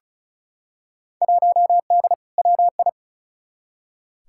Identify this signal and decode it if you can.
Morse code, heard as a keyed tone, about 36 wpm, 700 Hz. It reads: 1D WI
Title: Field Day Exchanges - 35wpm